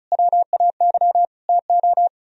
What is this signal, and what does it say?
Morse code, heard as a keyed tone, about 44 wpm, 700 Hz. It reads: WAY TO